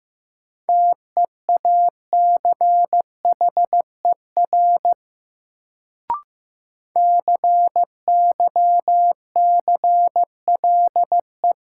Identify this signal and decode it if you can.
Morse code, heard as a keyed tone, about 15 wpm, 700 Hz. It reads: TEACHER CYCLE
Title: Sentences from Top 900 Words - New Word Drill - 15wpm